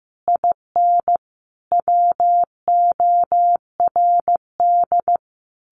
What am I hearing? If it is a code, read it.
Morse code, heard as a keyed tone, about 15 wpm, 700 Hz. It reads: IN WORD
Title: Sentences from Top 1200 Words 15wpm